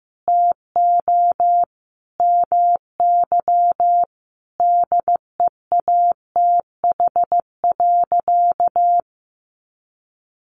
Morse code, heard as a keyed tone, about 15 wpm, 700 Hz. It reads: TO MY DEATH.
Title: Sentences from Top 1500 Words 15wpm